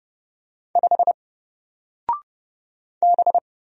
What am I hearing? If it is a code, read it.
Morse code, heard as a keyed tone, about 30 wpm, 700 Hz. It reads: 5 B
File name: Single-Letter-Number-30wpm